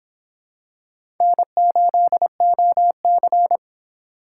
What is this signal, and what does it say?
Morse code, heard as a keyed tone, about 26 wpm, 700 Hz. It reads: N8OC